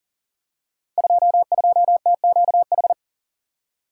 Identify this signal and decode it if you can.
Morse code, heard as a keyed tone, about 39 wpm, 700 Hz. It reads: 22TQH
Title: Code Groups - 5 Letter:Numbers - 40wpm